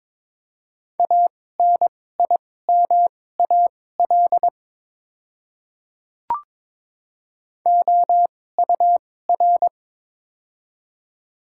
Morse code, heard as a keyed tone, about 22 wpm, spaced as for 16 wpm, 700 Hz. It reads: ANIMAL OUR
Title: 2x Character Spacing - Top 200 Words - 22wpm